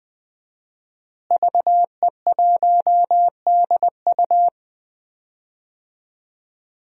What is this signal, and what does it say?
Morse code, heard as a keyed tone, about 20 wpm, 700 Hz. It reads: VE1DU
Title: DXCC-100 Call Signs - 5 Characters - Repeated 2x - 20wpm